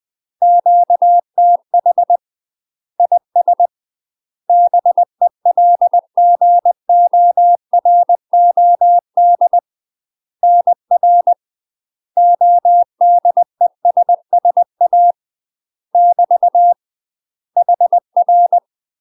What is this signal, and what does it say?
Morse code, heard as a keyed tone, about 20 wpm, 700 Hz. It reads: QTH IS BELGOROD NR ODESSA <BT> HR
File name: UK MASTER QSOs TRAINING/20WPM